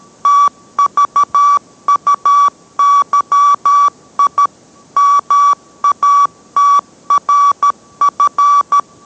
Morse code, heard as a keyed tone, about 13 wpm, 1150 Hz. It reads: TVUYI MATRF